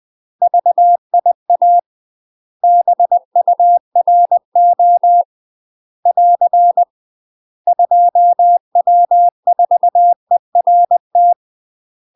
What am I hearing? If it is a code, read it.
Morse code, heard as a keyed tone, about 20 wpm, 700 Hz. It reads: VIA BURO <AR> 2W4ERT
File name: UK MASTER QSOs TRAINING / 20WPM / UK MASTER QSOs 0040